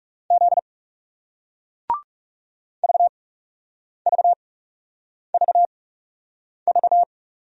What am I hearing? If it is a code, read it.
Morse code, heard as a keyed tone, about 38 wpm, spaced as for 8 wpm, 700 Hz. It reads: Z VVVV